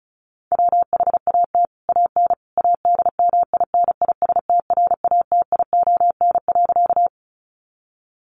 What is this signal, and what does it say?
Morse code, heard as a keyed tone, about 34 wpm, 700 Hz. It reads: WHAT AN ADMINISTRATION.
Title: Sentences from Top 1300 Words 35wpm